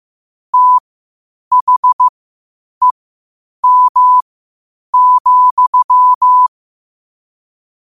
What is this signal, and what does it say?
Morse code, heard as a keyed tone, about 15 wpm, spaced as for 8 wpm, 1000 Hz. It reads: THEM,
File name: qotdfiles / 8wpm